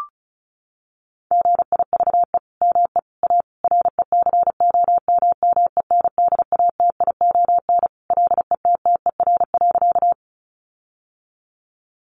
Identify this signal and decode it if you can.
Morse code, heard as a keyed tone, about 34 wpm, 700 Hz. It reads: GIVE ME A RECOMMENDATION LETTER.
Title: Sentences from Top 1400 Words 35wpm